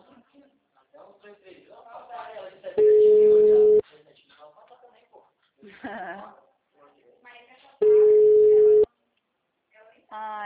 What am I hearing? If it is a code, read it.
Ringback tone (Europe, 425 Hz), with speech.